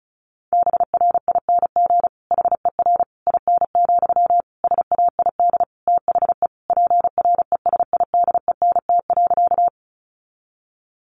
Morse code, heard as a keyed tone, about 34 wpm, 700 Hz. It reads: BRING HER IN, SAID THE PRESIDENT.
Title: Sentences from Top 1000 Words 35wpm